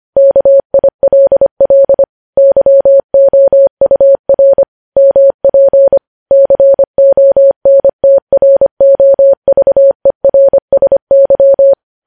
Morse code, heard as a keyed tone, about 25 wpm, 550 Hz. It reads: KILL YOUR MP CONTROVERSY
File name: podcast / News headlines-1x-25x25